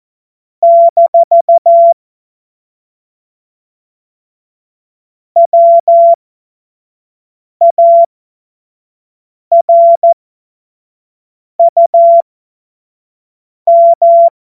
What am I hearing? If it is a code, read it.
Morse code, heard as a keyed tone, about 14 wpm, spaced as for 5 wpm, 680 Hz. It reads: - WARUM
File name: Morsekurs / Witze / Witz 17. Januar 2023 25 F70